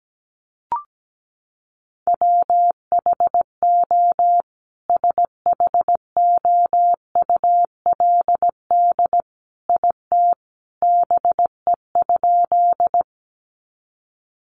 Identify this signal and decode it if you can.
Morse code, heard as a keyed tone, about 17 wpm, 700 Hz. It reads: WHO SHOULD IT BE?
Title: Sentences from Top 300 Words 17wpm